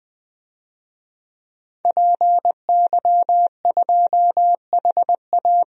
Morse code, heard as a keyed tone, about 20 wpm, 700 Hz. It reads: PY2HA
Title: DXCC-100 Call Signs - 5 Characters - 20wpm